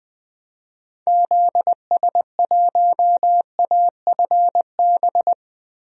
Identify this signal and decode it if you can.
Morse code, heard as a keyed tone, about 20 wpm, 700 Hz. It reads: ZS1AFB
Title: DXCC-100 Call Signs - 6 Characters - Repeated 3x - 20wpm